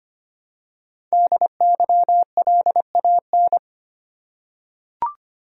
Morse code, heard as a keyed tone, about 25 wpm, 700 Hz. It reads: DYLAN
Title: USA Common Names - Max 5 Letters - 25wpm